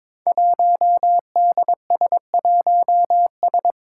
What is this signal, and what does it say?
Morse code, heard as a keyed tone, about 22 wpm, 700 Hz. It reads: 1DS1S
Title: Five Challenging Letter-Number Combinations 22wpm